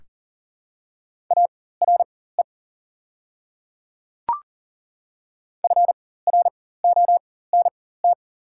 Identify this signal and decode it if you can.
Morse code, heard as a keyed tone, about 40 wpm, spaced as for 19 wpm, 700 Hz. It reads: ARE FRONT
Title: 4x Character Spacing - Top 500 Words - 40wpm